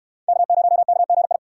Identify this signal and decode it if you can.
Morse code, heard as a keyed tone, about 63 wpm, 700 Hz. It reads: K0KGI